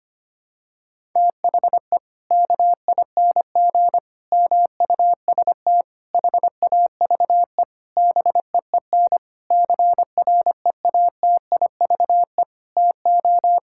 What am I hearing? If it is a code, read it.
Morse code, heard as a keyed tone, about 25 wpm, 700 Hz. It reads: THE KING MUST HAVE BEEN CREATIVE TO
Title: Sentences from Top 1000 Words - Review Part 5 - 25wpm